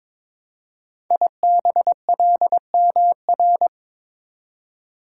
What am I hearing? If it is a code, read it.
Morse code, heard as a keyed tone, about 22 wpm, 700 Hz. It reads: IBLMR
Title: Code Groups - 5 Letters - 22wpm